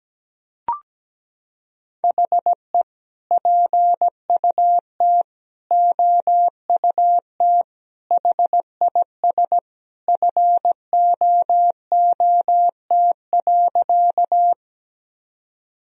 Morse code, heard as a keyed tone, about 17 wpm, 700 Hz. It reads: HE PUT OUT HIS FOOT.
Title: Sentences from Top 1000 Words - Review Part 3 - 17wpm